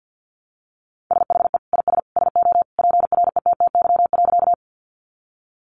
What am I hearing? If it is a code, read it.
Morse code, heard as a keyed tone, about 50 wpm, 700 Hz. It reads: SHE IS SO PRETTY.
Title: Sentences from Top 1000 Words 50wpm